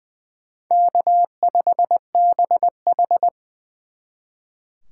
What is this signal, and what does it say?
Morse code, heard as a keyed tone, about 24 wpm, 700 Hz. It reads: K5BH
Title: Speed Racing - USA Call Signs 20wpm